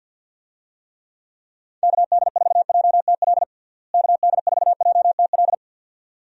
Morse code, heard as a keyed tone, about 52 wpm, 700 Hz. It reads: KD4JTL KD4JTL
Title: USA Call Signs - 6 Characters - Repeated 2x - 50wpm